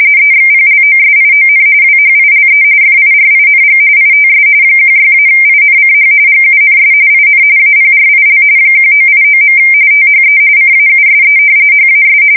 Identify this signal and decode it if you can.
RTTY, about 45.45 baud. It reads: EXPLOSION WHILE FIRING ITS 16-INCH GUNS IN APRIL 1989. THE CAUSE HAS
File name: fsk170 / battleshipnj